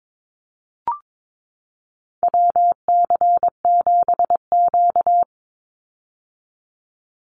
Morse code, heard as a keyed tone, about 22 wpm, 700 Hz. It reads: WC7Q
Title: Call Signs - 22wpm